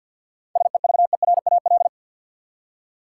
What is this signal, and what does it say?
Morse code, heard as a keyed tone, about 53 wpm, 700 Hz. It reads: SEVERAL
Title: Seven Letter Words - 50wpm